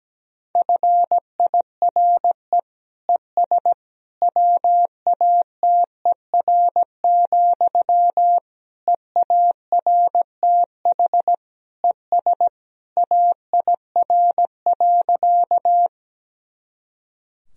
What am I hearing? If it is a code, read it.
Morse code, heard as a keyed tone, about 17 wpm, 700 Hz. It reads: FIRE ES WATER, EARTH ES AIR.
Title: Sentences from Top 400 Words and Abbreviations 17wpm